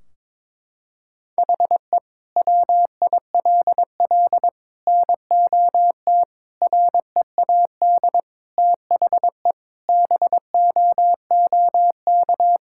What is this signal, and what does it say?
Morse code, heard as a keyed tone, about 22 wpm, 700 Hz. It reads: HE WILL NOT READ THE BOOK
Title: Sentences from Top 1600 Words 22wpm